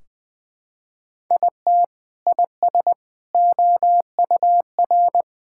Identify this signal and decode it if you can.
Morse code, heard as a keyed tone, about 21 wpm, 700 Hz. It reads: IT IS OUR
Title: Sentences from Top 1600 Words 20wpm